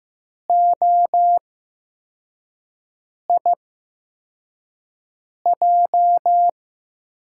Morse code, heard as a keyed tone, about 15 wpm, spaced as for 4 wpm, 700 Hz. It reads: OIJ
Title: USA Call Signs - 6 Characters - 8x Character Spacing - 15wpm